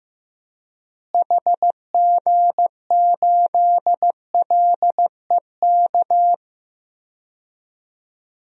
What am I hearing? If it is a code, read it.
Morse code, heard as a keyed tone, about 15 wpm, 700 Hz. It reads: HG8LEK
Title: DXCC-100 Call Signs - 6 Characters - Repeated 2x - 15wpm